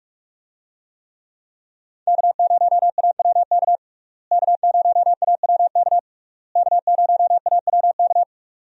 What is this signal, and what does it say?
Morse code, heard as a keyed tone, about 46 wpm, 700 Hz. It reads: K0AWK K0AWK K0AWK
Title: USA Call Signs - 5 Characters - Repeated 3x - 45wpm